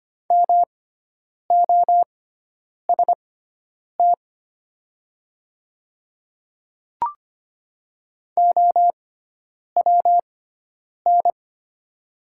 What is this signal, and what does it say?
Morse code, heard as a keyed tone, about 25 wpm, spaced as for 9 wpm, 700 Hz. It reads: MOST OWN
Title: 6x Character Spacing - Top 300 Words - 25wpm